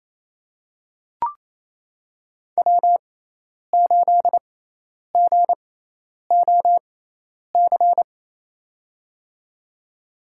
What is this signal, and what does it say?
Morse code, heard as a keyed tone, about 28 wpm, spaced as for 10 wpm, 700 Hz. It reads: W8GOC
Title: USA Call Signs - 6x Character Spacing - 28wpm